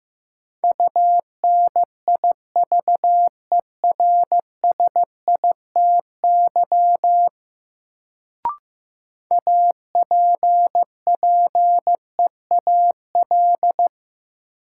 Morse code, heard as a keyed tone, about 15 wpm, 700 Hz. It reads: UNIVERSITY APPEAL
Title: Sentences from Top 1300 Words - New Word Drill Alt - 15wpm